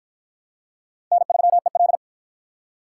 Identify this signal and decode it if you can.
Morse code, heard as a keyed tone, about 48 wpm, 700 Hz. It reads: N3EL